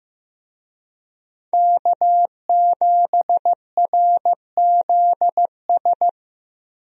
Morse code, heard as a keyed tone, about 15 wpm, 700 Hz. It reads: K7RZS
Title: USA Call Signs - 5 Characters - Repeated 3x - 15wpm